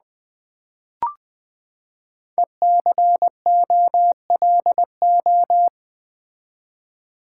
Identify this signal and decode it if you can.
Morse code, heard as a keyed tone, about 20 wpm, 700 Hz. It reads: ECOLO
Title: Build A Word - 5 Letters At A Time - 10-Letter Words - 20wpm